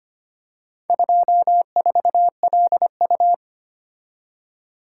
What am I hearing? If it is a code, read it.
Morse code, heard as a keyed tone, about 25 wpm, 700 Hz. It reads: 24LU
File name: practice / 25wpm / Four Letter-Number Combinations - 25wpm